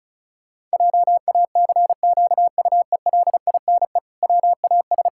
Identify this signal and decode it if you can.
Morse code, heard as a keyed tone, about 36 wpm, 700 Hz. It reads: JACQUELINE WAS
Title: Pangram Challenge v2 - 35wpm